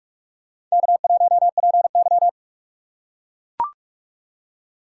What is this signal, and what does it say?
Morse code, heard as a keyed tone, about 41 wpm, 700 Hz. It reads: K1PY
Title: USA Call Signs - 4 Characters - 4x Character Spacing - 45wpm